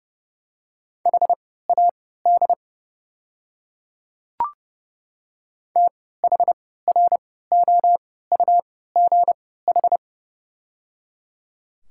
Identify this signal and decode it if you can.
Morse code, heard as a keyed tone, about 30 wpm, spaced as for 17 wpm, 700 Hz. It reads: HAD THROUGH
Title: 3x Character Spacing - Top 200 Words - 30wpm